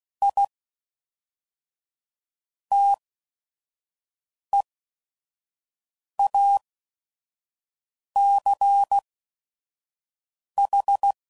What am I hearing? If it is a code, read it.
Morse code, heard as a keyed tone, about 16 wpm, spaced as for 5 wpm, 800 Hz. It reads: I TEACH
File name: CW Code Groups @5WPM / QSOs / WD0FBT 1